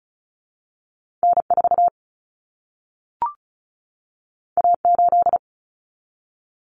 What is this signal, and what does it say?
Morse code, heard as a keyed tone, about 36 wpm, 700 Hz. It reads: N4 A8
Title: Common Cut Numbers - 2-Digit 35wpm